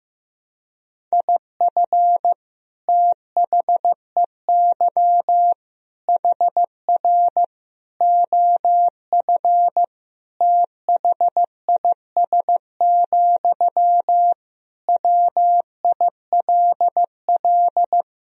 Morse code, heard as a keyed tone, about 15 wpm, 700 Hz. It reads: IF THEY HR OF THIS, WILL
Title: Sentences from Top 400 Words and Abbreviations 15wpm